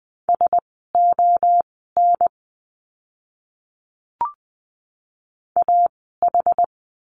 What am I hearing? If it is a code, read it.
Morse code, heard as a keyed tone, about 20 wpm, spaced as for 15 wpm, 700 Hz. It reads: SON AH
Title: 061 Lesson 11f - Letter h - Words 20wpm